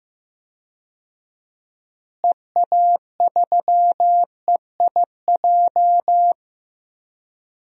Morse code, heard as a keyed tone, about 15 wpm, 700 Hz. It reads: EA3EIJ
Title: DXCC-100 Call Signs - 6 Characters - Repeated 3x - 15wpm